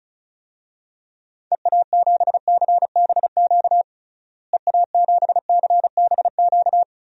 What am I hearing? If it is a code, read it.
Morse code, heard as a keyed tone, about 34 wpm, 700 Hz. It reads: EA7CBQ EA7CBQ
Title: DXCC-100 Call Signs - 6 Characters - Repeated 3x - 35wpm